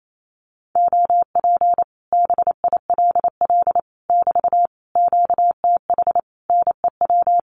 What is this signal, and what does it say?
Morse code, heard as a keyed tone, about 28 wpm, 700 Hz. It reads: OP BILL <BT> QTH NEW
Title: Common QSO Phrases Encore 28wpm